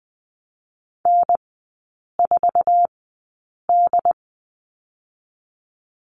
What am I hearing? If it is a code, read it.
Morse code, heard as a keyed tone, about 20 wpm, spaced as for 8 wpm, 700 Hz. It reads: N4D
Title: 067 Lesson 12e - Letter d - Three Characters 20wpm